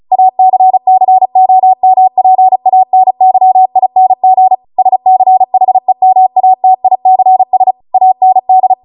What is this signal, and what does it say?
Morse code, heard as a keyed tone, about 36 wpm, 750 Hz. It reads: ACCOMPANYING SCHEMATICS AND